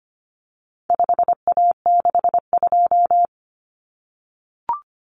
Morse code, heard as a keyed tone, about 25 wpm, 700 Hz. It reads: 5A62
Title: Common Cut Numbers - 4-Digit 25wpm